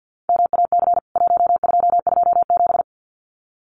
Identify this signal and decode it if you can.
Morse code, heard as a keyed tone, about 50 wpm, 700 Hz. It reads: MAC 1227
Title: CWT Exchange 50wpm